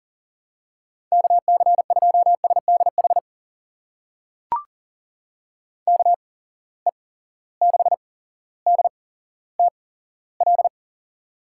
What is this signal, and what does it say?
Morse code, heard as a keyed tone, about 40 wpm, spaced as for 17 wpm, 700 Hz. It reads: KC2SDH KE6DTL
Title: USA Call Signs - 6 Characters - 8x Character Spacing - 40wpm